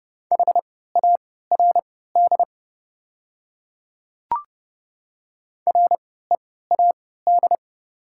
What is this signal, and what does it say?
Morse code, heard as a keyed tone, about 30 wpm, spaced as for 17 wpm, 700 Hz. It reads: HARD READ
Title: 3x Character Spacing - Top 500 Words - 30wpm